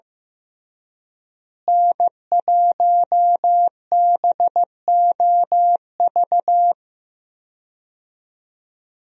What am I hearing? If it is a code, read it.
Morse code, heard as a keyed tone, about 15 wpm, 700 Hz. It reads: N1BOV
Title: USA Call Signs - 5 Characters - Repeated 2x - 15wpm